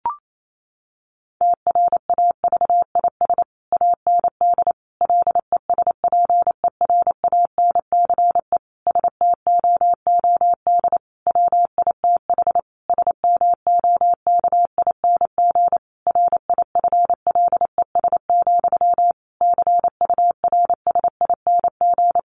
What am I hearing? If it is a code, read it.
Morse code, heard as a keyed tone, about 28 wpm, 700 Hz. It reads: TRAVIS AND LESPERANCE STOOD WITH SMOKING RIFLES, CURSING
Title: A Sound Of Thunder 28wpm